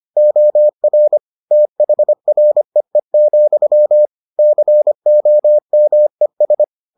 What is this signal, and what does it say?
Morse code, heard as a keyed tone, about 25 wpm, 600 Hz. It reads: OR THREE, COMES